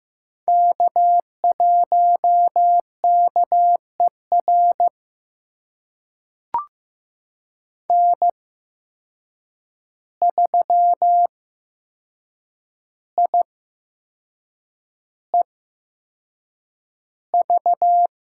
Morse code, heard as a keyed tone, about 15 wpm, spaced as for 6 wpm, 700 Hz. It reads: K1KER N3IEV
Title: USA Call Signs - 5 Characters - 8x Character Spacing - 15wpm